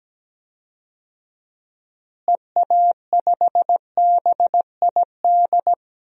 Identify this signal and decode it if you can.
Morse code, heard as a keyed tone, about 17 wpm, 700 Hz. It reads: EA5BID
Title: DXCC-100 Call Signs - 6 Characters - Repeated 2x - 17wpm